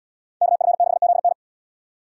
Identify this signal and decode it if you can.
Morse code, heard as a keyed tone, about 78 wpm, 700 Hz. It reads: KF6ZN